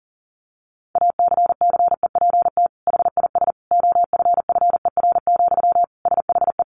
Morse code, heard as a keyed tone, about 40 wpm, 700 Hz. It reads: ACCEPT HIS OFFER, SHE
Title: Sentences from Top 1300 Words 40wpm